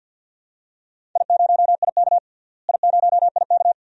Morse code, heard as a keyed tone, about 48 wpm, 700 Hz. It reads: I0IK I0IK
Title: DXCC-100 Call Signs - 4 Characters - Repeated 2x - 50wpm